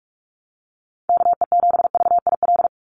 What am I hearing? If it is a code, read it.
Morse code, heard as a keyed tone, about 46 wpm, 700 Hz. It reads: KE7VIL
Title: USA Call Signs 45wpm